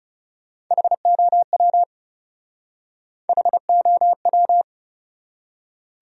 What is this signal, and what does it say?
Morse code, heard as a keyed tone, about 39 wpm, 700 Hz. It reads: HOW HOW